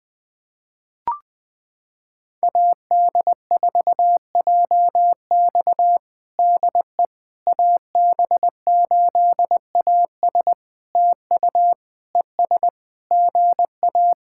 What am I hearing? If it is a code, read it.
Morse code, heard as a keyed tone, about 20 wpm, 700 Hz. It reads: AD4JX DE AB8AS TU ES GA
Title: POTA QSO Practice - 20wpm